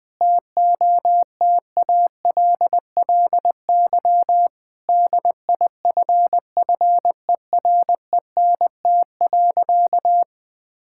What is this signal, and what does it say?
Morse code, heard as a keyed tone, about 20 wpm, 700 Hz. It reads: TOTALLY DIFFERENT.
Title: Sentences from Top 1600 Words 20wpm